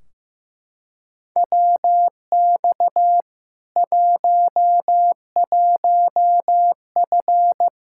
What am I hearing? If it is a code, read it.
Morse code, heard as a keyed tone, about 15 wpm, 700 Hz. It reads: WX 11F
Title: Practice QSO - Exchange 2 - Info - 15wpm